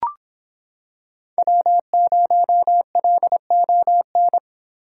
Morse code, heard as a keyed tone, about 26 wpm, 700 Hz. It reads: W0LON